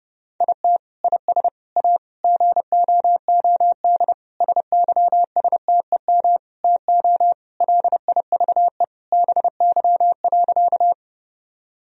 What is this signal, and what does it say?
Morse code, heard as a keyed tone, about 30 wpm, 700 Hz. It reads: IT IS A GOOD SYSTEM TO LIVE BY.
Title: Sentences from Top 1000 Words - Review Part 3 - 30wpm